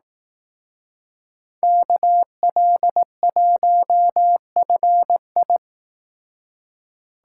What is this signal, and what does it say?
Morse code, heard as a keyed tone, about 18 wpm, 700 Hz. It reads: KL1FI
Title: Speed Racing - USA 2x2 Call Signs 15wpm